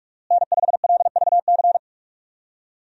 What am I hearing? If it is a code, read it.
Morse code, heard as a keyed tone, about 46 wpm, 700 Hz. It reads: N5LVC
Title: USA Call Signs - 5 Characters - Repeated 2x - 45wpm